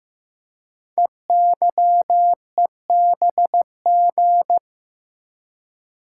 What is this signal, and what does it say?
Morse code, heard as a keyed tone, about 15 wpm, 700 Hz. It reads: EYEBG
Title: Code Groups - 5 Letters - 15wpm